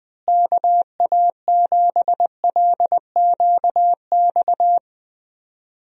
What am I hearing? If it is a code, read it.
Morse code, heard as a keyed tone, about 20 wpm, 700 Hz. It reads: KA7LQX
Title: USA Call Signs - 6 Characters - Repeated 2x - 20wpm